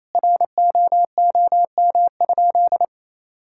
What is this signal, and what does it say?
Morse code, heard as a keyed tone, about 28 wpm, 700 Hz. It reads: ROOM?
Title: Sentences from Top 1000 Words - Review Part 5 - 28wpm